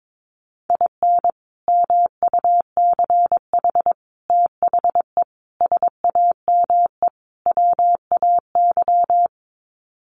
Morse code, heard as a keyed tone, about 22 wpm, 700 Hz. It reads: IN MUCH THE SAME WAY
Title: Sets of 5 Words Encore 22wpm